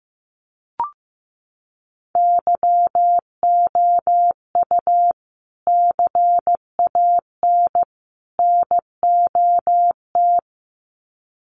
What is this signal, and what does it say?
Morse code, heard as a keyed tone, about 15 wpm, 700 Hz. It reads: YOU CAN NOT
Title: Sets of 3 Words 15wpm